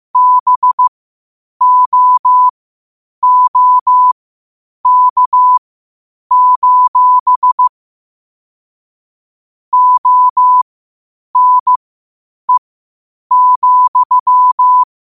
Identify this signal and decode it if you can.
Morse code, heard as a keyed tone, about 15 wpm, spaced as for 8 wpm, 1000 Hz. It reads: BOOK: ONE,
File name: qotdfiles / 8wpm